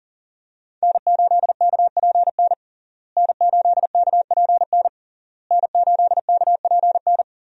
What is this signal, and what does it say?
Morse code, heard as a keyed tone, about 40 wpm, 700 Hz. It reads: N8KPN N8KPN N8KPN
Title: USA Call Signs - 5 Characters - Repeated 3x - 40wpm